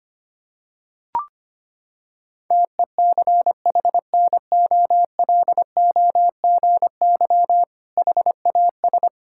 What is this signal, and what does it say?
Morse code, heard as a keyed tone, about 25 wpm, 700 Hz. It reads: TECHNOLOGY HAS
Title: Sentences from Top 1000 Words - Review Part 6 - 25wpm